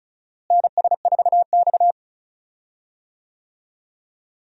Morse code, heard as a keyed tone, about 34 wpm, 700 Hz. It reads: NS4X
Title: USA Call Signs - 4 Characters - Repeated 3x - 35wpm